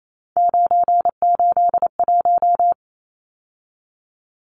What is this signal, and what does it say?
Morse code, heard as a keyed tone, about 28 wpm, 700 Hz. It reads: 981
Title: Numbers - 3-Digit 28wpm